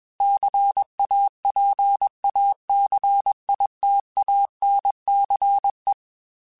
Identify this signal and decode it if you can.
Morse code, heard as a keyed tone, about 21 wpm, 800 Hz. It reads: CAPACITANCE
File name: Technical Subjects / Technical Subjects-mp3s / Reactance @20